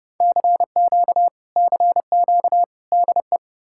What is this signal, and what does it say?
Morse code, heard as a keyed tone, about 40 wpm, 700 Hz. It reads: CQ CQ DE